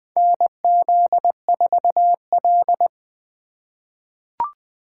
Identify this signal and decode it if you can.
Morse code, heard as a keyed tone, about 21 wpm, 700 Hz. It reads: NZ4L
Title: USA Call Signs - 4 Characters - 4x Character Spacing - 20wpm